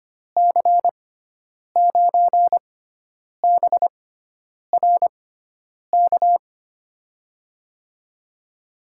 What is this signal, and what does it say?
Morse code, heard as a keyed tone, about 25 wpm, spaced as for 9 wpm, 700 Hz. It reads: C9BRK
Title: USA Call Signs - 6 Characters - 6x Character Spacing - 25wpm